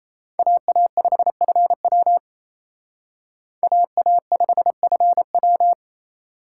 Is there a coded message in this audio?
Yes: Morse code from 0.4 to 5.7 s, heard as a keyed tone, about 35 wpm, 700 Hz. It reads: AA5FW AA5FW